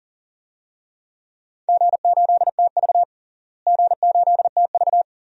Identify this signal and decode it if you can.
Morse code, heard as a keyed tone, about 38 wpm, 700 Hz. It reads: G8TV G8TV
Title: DXCC-100 Call Signs - 4 Characters - Repeated 2x - 40wpm